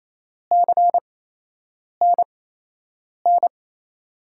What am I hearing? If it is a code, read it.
Morse code, heard as a keyed tone, about 28 wpm, spaced as for 8 wpm, 700 Hz. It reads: CNN
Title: USA Call Signs - 5 Characters - 8x Character Spacing - 28wpm